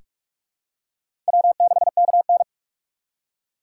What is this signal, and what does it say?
Morse code, heard as a keyed tone, about 43 wpm, 700 Hz. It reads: W6KN
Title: USA Call Signs - 3x Character Spacing - 45wpm